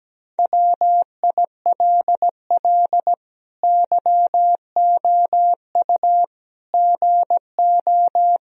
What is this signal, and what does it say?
Morse code, heard as a keyed tone, about 17 wpm, 700 Hz. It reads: WILL YOU GO
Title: Sentences from Top 1000 Words - Review Part 2 - 17wpm